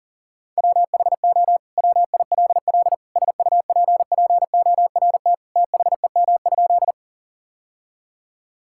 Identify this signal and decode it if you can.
Morse code, heard as a keyed tone, about 40 wpm, 700 Hz. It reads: WHO WILL SUPPORT THEM?
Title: Sentences from Top 1000 Words - Review Part 5 - 40wpm